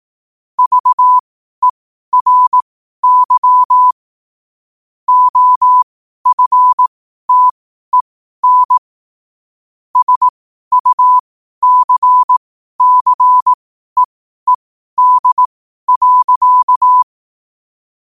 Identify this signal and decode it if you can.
Morse code, heard as a keyed tone, about 18 wpm, spaced as for 12 wpm, 1000 Hz. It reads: VERY OFTEN SUCCEED.
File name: qotdfiles / 12wpm